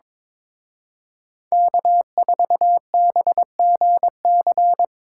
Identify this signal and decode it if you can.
Morse code, heard as a keyed tone, about 25 wpm, 700 Hz. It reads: K4BGC
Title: Speed Racing - USA 1x3 Call Signs 22wpm